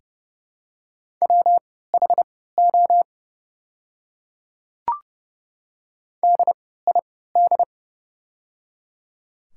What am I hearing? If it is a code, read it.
Morse code, heard as a keyed tone, about 30 wpm, spaced as for 17 wpm, 700 Hz. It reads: WHO DID
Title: 3x Character Spacing - Top 100 Words - 30wpm